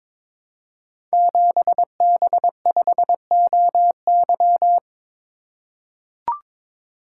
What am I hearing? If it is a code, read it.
Morse code, heard as a keyed tone, about 22 wpm, 700 Hz. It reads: 7B5OY
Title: Code Groups - 5 Letter:Numbers - 22wpm